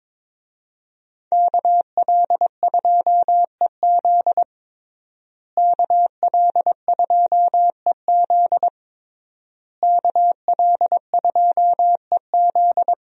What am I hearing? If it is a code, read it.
Morse code, heard as a keyed tone, about 22 wpm, 700 Hz. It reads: KL2EZ KL2EZ KL2EZ
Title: USA Call Signs - 5 Characters - Repeated 3x - 22wpm